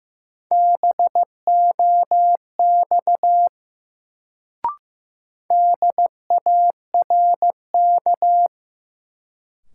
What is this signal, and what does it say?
Morse code, heard as a keyed tone, about 15 wpm, 700 Hz. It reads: BOX DARK
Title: Sentences from Top 500 Words - New Word Drill - 15wpm